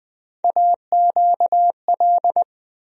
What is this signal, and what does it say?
Morse code, heard as a keyed tone, about 20 wpm, 700 Hz. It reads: AQL
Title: USA Call Sign Postfixes - 3 Letters 20wpm